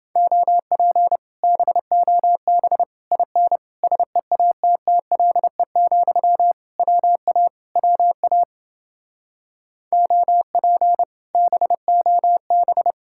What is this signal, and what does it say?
Morse code, heard as a keyed tone, about 29 wpm, 700 Hz. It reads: OP BOB IN SEATTLE, WA WA OP BOB